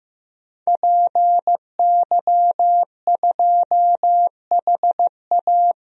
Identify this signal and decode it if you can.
Morse code, heard as a keyed tone, about 15 wpm, 700 Hz. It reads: PY2HA
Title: DXCC-100 Call Signs - 5 Characters - 15wpm